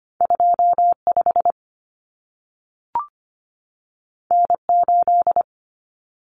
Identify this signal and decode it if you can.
Morse code, heard as a keyed tone, about 25 wpm, 700 Hz. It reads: 25 N8
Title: Common Cut Numbers - 2-Digit 25wpm